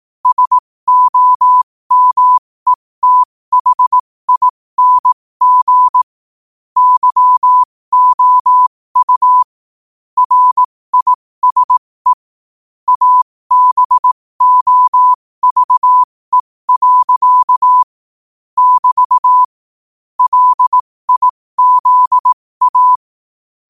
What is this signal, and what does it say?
Morse code, heard as a keyed tone, about 18 wpm, spaced as for 15 wpm, 1000 Hz. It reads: SOMETHING YOU RISE ABOVE. <BT> LIZA
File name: qotdfiles / 15wpm